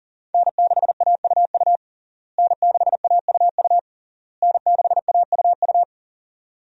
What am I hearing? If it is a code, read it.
Morse code, heard as a keyed tone, about 37 wpm, 700 Hz. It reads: N6AUU N6AUU N6AUU
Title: USA Call Signs - 5 Characters - Repeated 3x - 40wpm